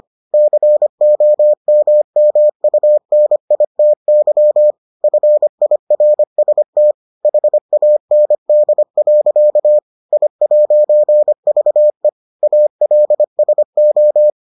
Morse code, heard as a keyed tone, about 25 wpm, 600 Hz. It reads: COMMUNITY FIRST HAND. I'VE ALSO